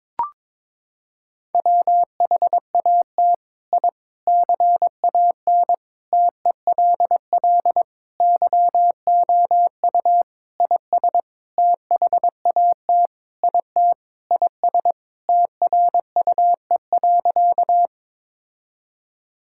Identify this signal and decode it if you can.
Morse code, heard as a keyed tone, about 22 wpm, 700 Hz. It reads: WHAT I CAN TELL YOU IS THAT IT IS TRUE.
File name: Sentences with 7-Word Phrases - 22wpm